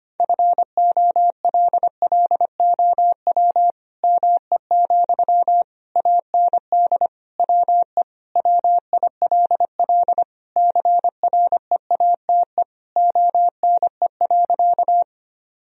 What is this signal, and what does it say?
Morse code, heard as a keyed tone, about 25 wpm, 700 Hz. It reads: FOLLOW ME, AND WE WILL CREATE ONE.
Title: Sentences from Top 1000 Words - Review Part 3 - 25wpm